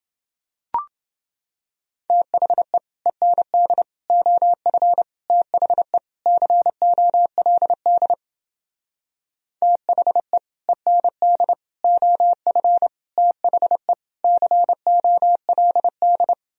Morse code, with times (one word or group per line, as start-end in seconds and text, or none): 2.1-2.8 THE
3.1-3.8 END
4.1-5.0 OF
5.3-6.0 THE
6.3-8.1 COLD
9.6-10.4 THE
10.7-11.5 END
11.8-12.9 OF
13.2-13.9 THE
14.2-16.3 COLD